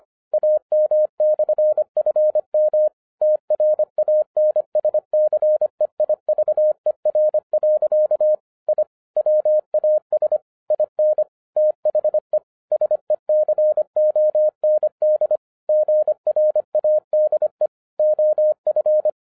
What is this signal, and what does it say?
Morse code, heard as a keyed tone, about 25 wpm, 610 Hz. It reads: AM/FM TRANSCEIVER. I WAS IN THE SECOND GRADE OF